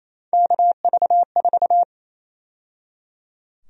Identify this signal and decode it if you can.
Morse code, heard as a keyed tone, about 28 wpm, 700 Hz. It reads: KV4
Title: USA Call Sign Prefixes - 28wpm